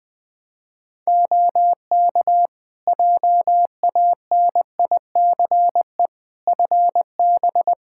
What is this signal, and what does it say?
Morse code, heard as a keyed tone, about 20 wpm, 700 Hz. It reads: OK JANICE FB
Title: K1USN SST QSO Practice 20wpm